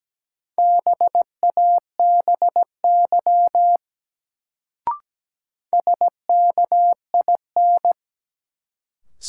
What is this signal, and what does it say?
Morse code, heard as a keyed tone, about 17 wpm, 700 Hz. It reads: BABY SKIN
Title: Sentences from Top 1300 Words - New Word Drill Alt - 17wpm